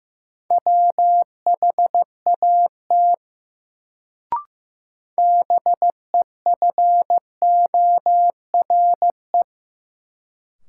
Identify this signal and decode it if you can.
Morse code, heard as a keyed tone, about 15 wpm, 700 Hz. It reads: WHAT BEFORE
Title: Sets of 2 Words - New Word Drill - 15wpm